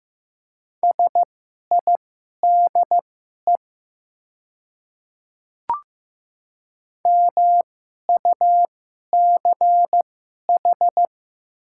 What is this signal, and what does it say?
Morse code, heard as a keyed tone, about 15 wpm, spaced as for 11 wpm, 700 Hz. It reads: SIDE MUCH
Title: 2x Character Spacing - Top 200 Words - 15wpm